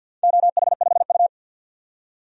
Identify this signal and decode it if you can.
Morse code, heard as a keyed tone, about 51 wpm, 700 Hz. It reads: OH5U
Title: DXCC-100 Call Signs - 4 Characters - 50wpm